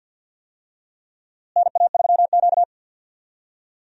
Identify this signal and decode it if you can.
Morse code, heard as a keyed tone, about 46 wpm, 700 Hz. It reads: NA3Q
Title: USA Call Signs - 4 Characters - Repeated 2x - 50wpm